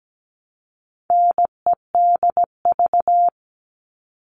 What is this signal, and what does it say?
Morse code, heard as a keyed tone, about 17 wpm, 700 Hz. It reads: NEDV
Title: All Cut Numbers - 4-Digit 17wpm